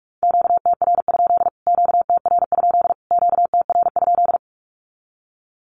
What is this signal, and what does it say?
Morse code, heard as a keyed tone, about 44 wpm, 700 Hz. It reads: QTR? QTR? QTR?
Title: Q-Codes Repeated 3x 45wpm